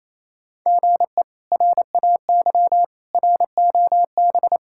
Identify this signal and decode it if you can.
Morse code, heard as a keyed tone, about 28 wpm, 700 Hz. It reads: GE RAY ROB
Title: K1USN SST QSO Practice - 28wpm